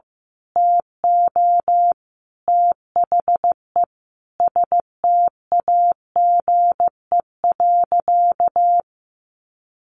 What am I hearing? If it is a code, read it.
Morse code, heard as a keyed tone, about 15 wpm, 700 Hz. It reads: TO THE STAGE.
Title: Sentences from Top 1200 Words 15wpm